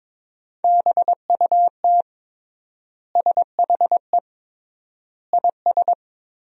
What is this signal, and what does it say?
Morse code, heard as a keyed tone, about 22 wpm, 700 Hz. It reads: BUT SHE IS
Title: 3x Word-Spacing - Sentences from Top 500 Words - 22wpm